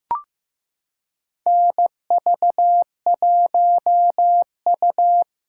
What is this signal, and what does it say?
Morse code, heard as a keyed tone, about 15 wpm, 700 Hz. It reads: NV1U
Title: K1USN SST QSO Practice - 15wpm